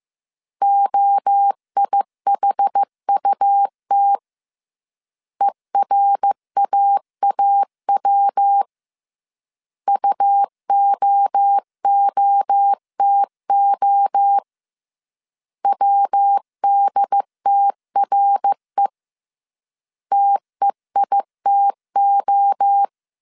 Morse code, heard as a keyed tone, about 15 wpm, 800 Hz. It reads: OIHUT ERAAW UOOTO WDTRE TEITO